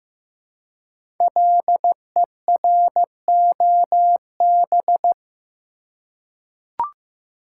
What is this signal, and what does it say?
Morse code, heard as a keyed tone, about 15 wpm, 700 Hz. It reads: LEROB
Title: Code Groups - 5 Letters - 15wpm